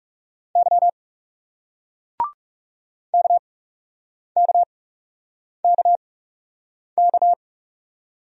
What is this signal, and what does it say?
Morse code, heard as a keyed tone, about 37 wpm, spaced as for 8 wpm, 700 Hz. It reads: Y KKKK
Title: Speed-Racing Single Letter-Number 30wpm